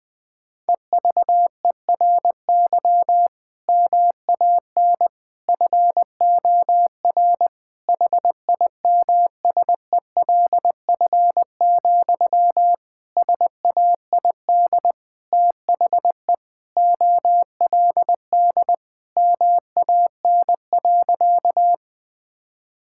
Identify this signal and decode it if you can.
Morse code, heard as a keyed tone, about 20 wpm, 700 Hz. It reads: EVERY MAN FOR HIMSELF, SAID THE OLD MAN.
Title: Sentences from Top 1600 Words 20wpm